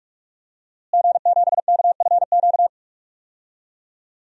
Morse code, heard as a keyed tone, about 46 wpm, 700 Hz. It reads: G7KFQ
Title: DXCC-100 Call Signs - 5 Characters - Repeated 2x - 45wpm